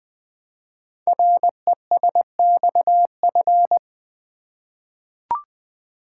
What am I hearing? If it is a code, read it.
Morse code, heard as a keyed tone, about 20 wpm, 700 Hz. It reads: RESXF
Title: Code Groups - 5 Letters - 20wpm